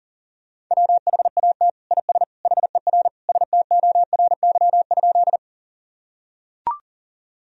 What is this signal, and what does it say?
Morse code, heard as a keyed tone, about 37 wpm, 700 Hz. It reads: WHAT IS HER STORY?
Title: Sentences from Top 1000 Words - Review Part 1 - 40wpm